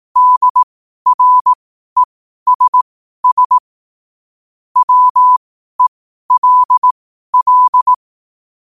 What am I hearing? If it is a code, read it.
Morse code, heard as a keyed tone, about 18 wpm, spaced as for 12 wpm, 1000 Hz. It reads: DRESS WELL
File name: qotdfiles / 12wpm